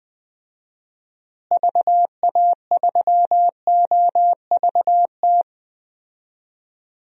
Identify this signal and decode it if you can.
Morse code, heard as a keyed tone, about 20 wpm, 700 Hz. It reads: VA3OVT
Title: DXCC-100 Call Signs - 6 Characters - Repeated 3x - 20wpm